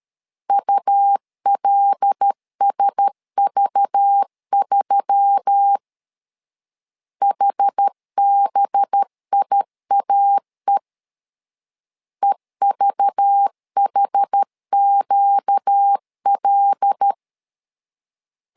Morse code, heard as a keyed tone, about 13 wpm, 800 Hz. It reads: ULSV3 HBIAE EVHQL
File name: LEZIONI / 5 LBVC / 5 LBV 12wpm 1855